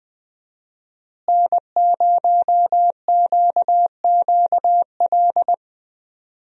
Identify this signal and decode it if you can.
Morse code, heard as a keyed tone, about 20 wpm, 700 Hz. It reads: N0QQL
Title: USA Call Signs - 5 Characters - Repeated 3x - 20wpm